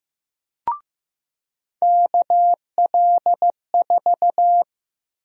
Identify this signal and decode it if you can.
Morse code, heard as a keyed tone, about 15 wpm, 700 Hz. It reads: KL4
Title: USA Call Sign Prefixes - 15wpm